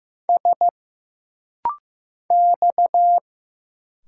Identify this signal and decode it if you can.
Morse code, heard as a keyed tone, about 15 wpm, 700 Hz. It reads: S X
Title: Single Letters - Rapid-Fire 15wpm